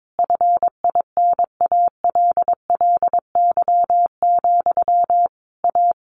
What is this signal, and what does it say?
Morse code, heard as a keyed tone, about 22 wpm, 700 Hz. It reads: FINALLY, A
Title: A Sound Of Thunder 22wpm